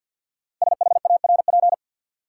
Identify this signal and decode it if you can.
Morse code, heard as a keyed tone, about 53 wpm, 700 Hz. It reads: SHARP
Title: Sentences from Top 1600 Words - New Word Drill - 50wpm